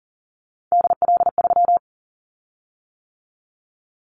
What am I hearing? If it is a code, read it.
Morse code, heard as a keyed tone, about 37 wpm, 700 Hz. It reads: DL3
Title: Callsign Prefixes for Top 25 DXCC - 40wpm